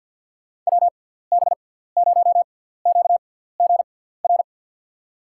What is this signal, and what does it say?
Morse code, heard as a keyed tone, about 46 wpm, spaced as for 17 wpm, 700 Hz. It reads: WB0QGR
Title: USA Call Signs - 6 Characters - 6x Character Spacing - 50wpm